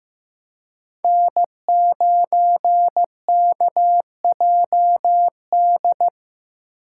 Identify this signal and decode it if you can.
Morse code, heard as a keyed tone, about 15 wpm, 700 Hz. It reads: N9KJD
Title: USA Call Signs - 5 Characters - 3x Character Spacing - 15wpm